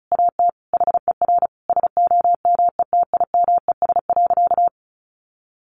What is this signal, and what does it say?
Morse code, heard as a keyed tone, about 34 wpm, 700 Hz. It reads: AT HER SOMETIMES.
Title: Sentences from Top 600 Words 35wpm